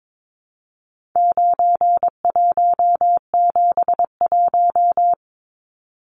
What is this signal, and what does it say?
Morse code, heard as a keyed tone, about 22 wpm, 700 Hz. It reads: 9171
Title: Numbers - 4-Digit 22wpm